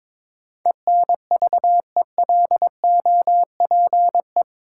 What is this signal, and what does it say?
Morse code, heard as a keyed tone, about 21 wpm, 700 Hz. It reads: ENVELOPE
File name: practice / 22wpm / Eight Letter Words - 22wpm